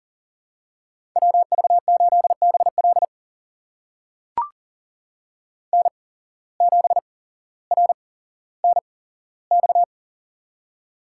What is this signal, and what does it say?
Morse code, heard as a keyed tone, about 40 wpm, spaced as for 17 wpm, 700 Hz. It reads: WV8BL N7RNX
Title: USA Call Signs - 5 Characters - 8x Character Spacing - 40wpm